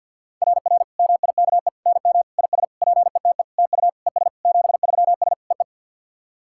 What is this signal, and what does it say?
Morse code, heard as a keyed tone, about 49 wpm, 690 Hz. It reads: RR MIKE NM IS PETE TU ES 73S EE
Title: SOTA QSO Practice 50wpm